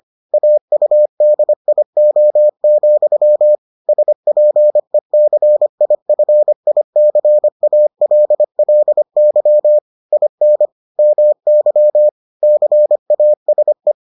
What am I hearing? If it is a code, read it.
Morse code, heard as a keyed tone, about 25 wpm, 600 Hz. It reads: AUDIO, SPECIFICALLY IN MY CASE